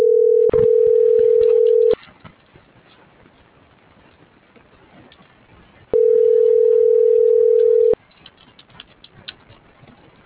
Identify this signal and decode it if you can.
Ringback tone (North America), 440 + 480 Hz.